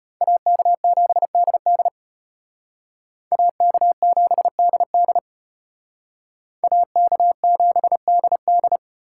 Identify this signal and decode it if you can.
Morse code, heard as a keyed tone, about 31 wpm, 700 Hz. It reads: AK7DD AK7DD AK7DD